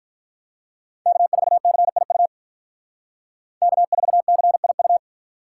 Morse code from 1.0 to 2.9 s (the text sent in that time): K4CIU